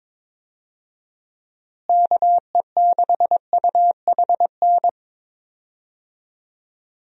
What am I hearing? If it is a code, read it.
Morse code, heard as a keyed tone, about 22 wpm, 700 Hz. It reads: KE6UHN
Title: USA Call Signs - 6 Characters - Repeated 3x - 22wpm